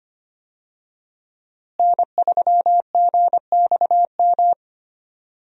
Morse code, heard as a keyed tone, about 25 wpm, 700 Hz. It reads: N3GXM
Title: USA Call Signs - 5 Characters - Repeated 2x - 25wpm